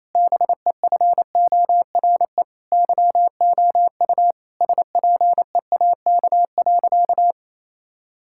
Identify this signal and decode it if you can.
Morse code, heard as a keyed tone, about 28 wpm, 700 Hz. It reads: BEFORE YOU SPEAK.
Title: Sentences from Top 1600 Words 28wpm